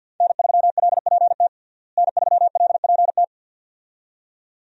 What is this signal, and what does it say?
Morse code, heard as a keyed tone, about 52 wpm, 700 Hz. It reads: N3LPT N3LPT
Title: USA Call Signs - 5 Characters - Repeated 2x - 50wpm